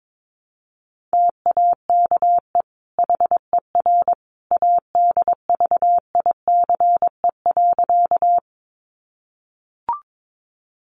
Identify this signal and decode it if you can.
Morse code, heard as a keyed tone, about 22 wpm, 700 Hz. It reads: TAKE HER ADVICE.
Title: Sentences from Top 1100 Words 22wpm